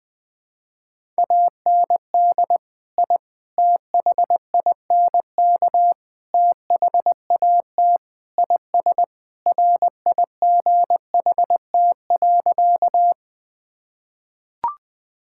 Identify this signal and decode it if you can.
Morse code, heard as a keyed tone, about 20 wpm, 700 Hz. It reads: AND I THINK THAT IS RIGHT.
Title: Sentences with 4-Word Phrases - 20wpm